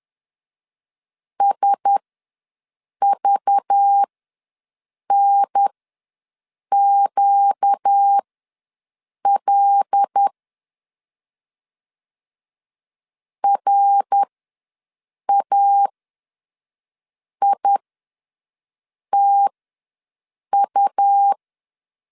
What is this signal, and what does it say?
Morse code, heard as a keyed tone, about 10 wpm, spaced as for 6 wpm, 800 Hz. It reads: SVNQL RAITU
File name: LEZIONI / 6 PKX0 / 6 P ws 10wpm 3751